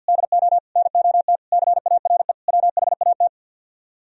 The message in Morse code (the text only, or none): DO NOT CARE WHAT